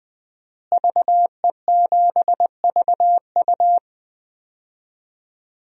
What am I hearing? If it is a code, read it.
Morse code, heard as a keyed tone, about 20 wpm, 700 Hz. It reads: VE7VU
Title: DXCC-100 Call Signs - 5 Characters - Repeated 3x - 20wpm